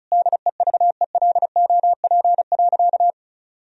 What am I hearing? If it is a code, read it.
Morse code, heard as a keyed tone, about 34 wpm, 700 Hz. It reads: DEVELOP.
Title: Sentences from Top 1000 Words - Review Part 2 - 35wpm